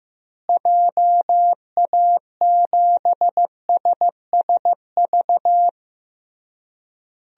Morse code, heard as a keyed tone, about 15 wpm, 700 Hz. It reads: JA7SSV
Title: DXCC-100 Call Signs - 6 Characters - Repeated 3x - 15wpm